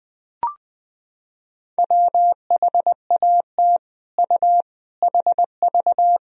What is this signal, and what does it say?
Morse code, heard as a keyed tone, about 20 wpm, 700 Hz. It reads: WHAT U HV
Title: Sentences from Top 200 Words and Abbreviations 20wpm